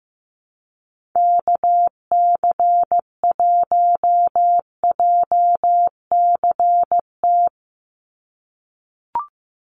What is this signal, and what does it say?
Morse code, heard as a keyed tone, about 15 wpm, 700 Hz. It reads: KC1JCT
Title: USA Call Signs 15wpm